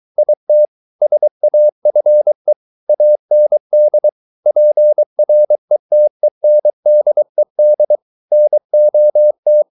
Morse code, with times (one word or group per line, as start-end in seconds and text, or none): 0.2-0.7 IT
1.0-2.5 SAFE
2.9-4.1 AND
4.5-8.0 PRETENDED
8.3-9.6 NOT